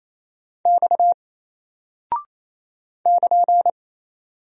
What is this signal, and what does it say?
Morse code, heard as a keyed tone, about 27 wpm, 700 Hz. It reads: X <KN>
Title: Single Character (Letter:Number:Punctuation:Prosign) Alt 28wpm